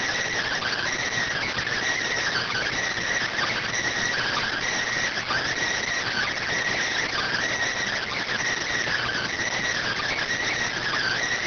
Emergency Alert System (SAME) tones (header bursts), heard as harsh buzzing bursts. It is heard from 0.6 to 11.2 s.